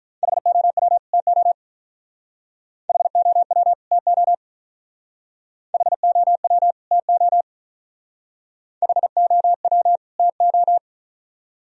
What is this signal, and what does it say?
Morse code, heard as a keyed tone, about 45 wpm, 700 Hz. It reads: HOW TO HOW TO HOW TO HOW TO